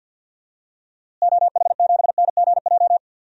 Morse code, heard as a keyed tone, about 53 wpm, 700 Hz. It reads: OH7NGJ
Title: DXCC-100 Call Signs - 6 Characters - Repeated 3x - 50wpm